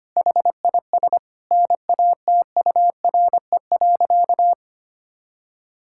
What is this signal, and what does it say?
Morse code, heard as a keyed tone, about 25 wpm, 700 Hz. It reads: HIS NATURE.
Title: Sentences from Top 1000 Words - Review Part 5 - 25wpm